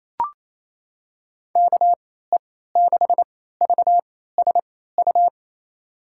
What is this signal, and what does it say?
Morse code, heard as a keyed tone, about 28 wpm, spaced as for 16 wpm, 700 Hz. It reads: KE6VSU
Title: USA Call Signs - 6 Characters - 3x Character Spacing - 28wpm